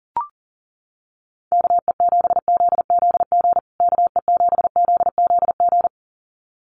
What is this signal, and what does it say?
Morse code, heard as a keyed tone, about 40 wpm, 700 Hz. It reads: KE7ZZG KE7ZZG
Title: USA Call Signs Repeated 2x 40wpm